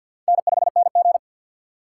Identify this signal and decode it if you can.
Morse code, heard as a keyed tone, about 52 wpm, 700 Hz. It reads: N5NG
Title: USA Call Signs - 4 Characters - 3x Character Spacing - 50wpm